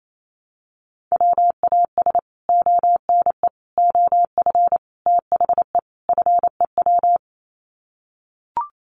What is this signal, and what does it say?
Morse code, heard as a keyed tone, about 28 wpm, 700 Hz. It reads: WAS ONE OF THE FEW
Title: Sets of 5 Words Encore 28wpm